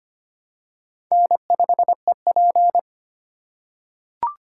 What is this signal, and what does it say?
Morse code, heard as a keyed tone, about 25 wpm, 700 Hz. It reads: N5EP
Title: USA Call Signs - 4 Characters - 6x Character Spacing - 25wpm